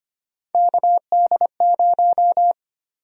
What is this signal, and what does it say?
Morse code, heard as a keyed tone, about 25 wpm, 700 Hz. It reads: KD0
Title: USA Call Sign Prefixes - 25wpm